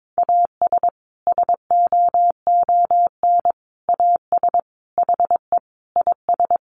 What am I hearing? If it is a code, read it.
Morse code, heard as a keyed tone, about 22 wpm, 700 Hz. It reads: AS SOON AS HE IS
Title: Sentences from Top 1500 Words 22wpm